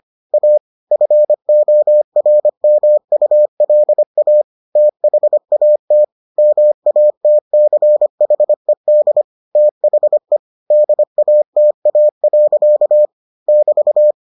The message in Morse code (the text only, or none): A FORMULA THAT MATCHED THE DATA. <BT>